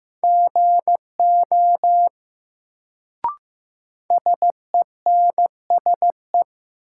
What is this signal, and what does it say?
Morse code, heard as a keyed tone, about 15 wpm, 700 Hz. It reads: GO SENSE
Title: Sets of 2 Words - New Word Drill - 15wpm